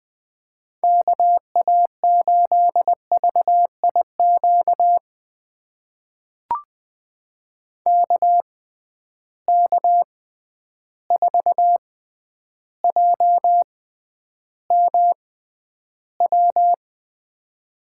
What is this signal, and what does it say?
Morse code, heard as a keyed tone, about 20 wpm, spaced as for 10 wpm, 700 Hz. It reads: KA8VIQ KK4JMW
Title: USA Call Signs - 6 Characters - 6x Character Spacing - 20wpm